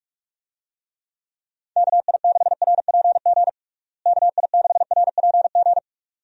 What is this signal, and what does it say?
Morse code, heard as a keyed tone, about 42 wpm, 700 Hz. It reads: KI6RPG KI6RPG
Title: USA Call Signs - 6 Characters - Repeated 2x - 45wpm